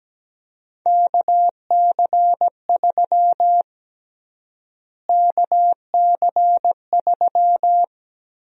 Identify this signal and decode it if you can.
Morse code, heard as a keyed tone, about 17 wpm, 700 Hz. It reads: KC3 KC3
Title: USA Call Sign Prefixes - Repeated 2x - 17wpm